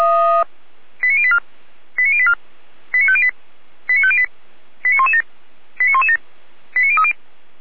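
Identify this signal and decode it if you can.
Five-tone selective call (ZVEI), heard as quick runs of five tones, tones of about 70 ms. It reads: 89084 89084 89508 89508 89117 89117 89030
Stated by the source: ZVEI five-tone selective call received with an SDR.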